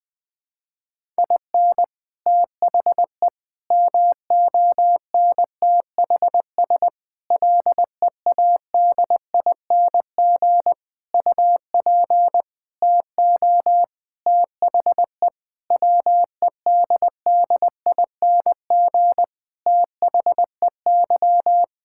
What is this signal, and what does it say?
Morse code, heard as a keyed tone, about 20 wpm, 700 Hz. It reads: IN THE MONTHS LEADING UP TO THE WEDDING THEY
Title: Sentences with 6-Word Phrases - Spoken Beginning - 20wpm